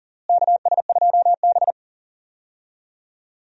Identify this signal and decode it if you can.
Morse code, heard as a keyed tone, about 40 wpm, 700 Hz. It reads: KS2B
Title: USA Call Signs - 4 Characters - Repeated 3x - 40wpm